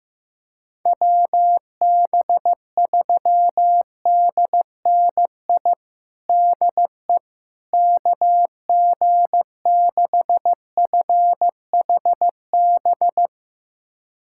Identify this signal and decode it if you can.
Morse code, heard as a keyed tone, about 15 wpm, 700 Hz. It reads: WB3DNI DE KG6FHB
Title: USA Callsign DE Callsign - 6 Characters - 15wpm